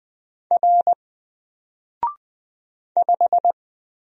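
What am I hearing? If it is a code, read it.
Morse code, heard as a keyed tone, about 20 wpm, 700 Hz. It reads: R 5
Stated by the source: Single letter and number Morse drill at 20 wpm, from N4LCD.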